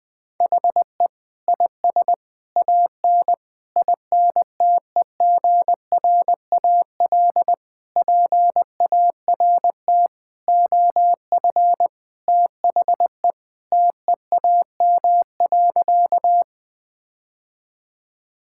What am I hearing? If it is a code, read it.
Morse code, heard as a keyed tone, about 20 wpm, 700 Hz. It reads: HE IS AN INTEGRAL PART OF THE TEAM.
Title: Sentences with 5-Word Phrases - 20wpm